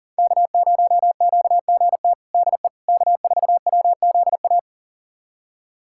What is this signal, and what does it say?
Morse code, heard as a keyed tone, about 40 wpm, 700 Hz. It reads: K0QGT DE K4WZA
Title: USA Callsign DE Callsign - 5 Characters - 40wpm